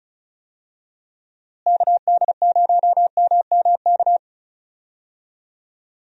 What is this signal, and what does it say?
Morse code, heard as a keyed tone, about 36 wpm, 700 Hz. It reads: KD0MMK
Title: USA Call Signs - 6 Characters - Repeated 3x - 35wpm